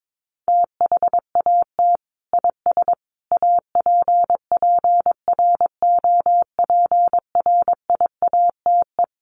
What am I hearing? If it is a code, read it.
Morse code, heard as a keyed tone, about 22 wpm, 700 Hz. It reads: THAT IS APPROPRIATE
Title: Sentences from Top 900 Words 22wpm